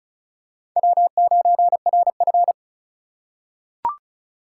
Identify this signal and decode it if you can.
Morse code, heard as a keyed tone, about 36 wpm, 700 Hz. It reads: W9RF
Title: USA Call Signs - 4 Characters - 4x Character Spacing - 35wpm